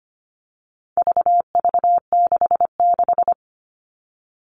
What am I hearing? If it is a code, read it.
Morse code, heard as a keyed tone, about 25 wpm, 700 Hz. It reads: VV66
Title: All Cut Numbers - 4-Digit 25wpm